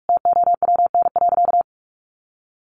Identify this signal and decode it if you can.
Morse code, heard as a keyed tone, about 44 wpm, 700 Hz. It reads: TOWN.
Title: Sentences from Top 1300 Words 45wpm